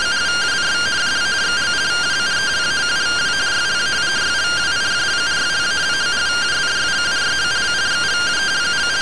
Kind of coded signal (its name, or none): two-tone teleprinter signal